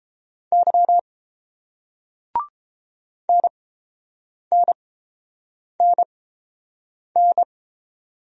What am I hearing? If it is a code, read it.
Morse code, heard as a keyed tone, about 30 wpm, spaced as for 7 wpm, 700 Hz. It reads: Y NNNN